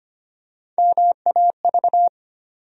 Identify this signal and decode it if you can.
Morse code, heard as a keyed tone, about 25 wpm, 700 Hz. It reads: MAV
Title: USA Call Sign Postfixes - 3 Letters 25wpm